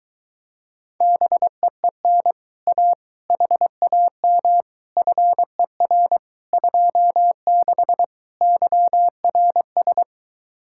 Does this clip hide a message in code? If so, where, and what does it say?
Yes: Morse code from 1.0 to 10.0 s, heard as a keyed tone, about 23 wpm, 700 Hz. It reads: BEEN A HAM FER 26 YRS